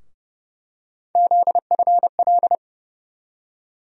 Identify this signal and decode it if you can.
Morse code, heard as a keyed tone, about 30 wpm, 700 Hz. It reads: ZFL
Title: USA Call Sign Postfixes - 3 Letters 30wpm